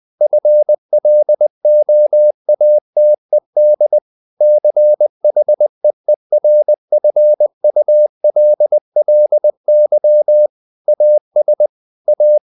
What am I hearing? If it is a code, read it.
Morse code, heard as a keyed tone, about 20 wpm, 600 Hz. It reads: FLOATED CHEERFULLY AS A
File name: the old man and the sea / the old man and the sea part 1 20wpm